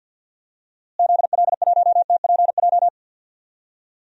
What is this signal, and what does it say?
Morse code, heard as a keyed tone, about 46 wpm, 700 Hz. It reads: ZL1TPJ
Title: DXCC-100 Call Signs - 6 Characters - Repeated 3x - 50wpm